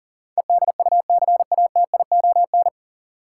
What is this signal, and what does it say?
Morse code, heard as a keyed tone, about 40 wpm, 700 Hz. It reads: EDUCATION
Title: Sentences from Top 1100 Words - New Word Drill Alt - 40wpm